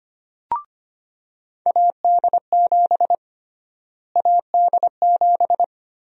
Morse code, heard as a keyed tone, about 25 wpm, 700 Hz. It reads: AD7 AD7
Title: USA Call Sign Prefixes - Repeated 2x - 25wpm